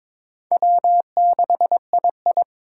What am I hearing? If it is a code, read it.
Morse code, heard as a keyed tone, about 22 wpm, 700 Hz. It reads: W6II
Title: USA Call Signs - 3x Character Spacing - 22wpm